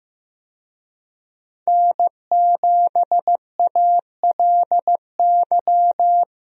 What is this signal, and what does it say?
Morse code, heard as a keyed tone, about 15 wpm, 700 Hz. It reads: N7ALY
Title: USA Call Signs - 5 Characters - Repeated 2x - 15wpm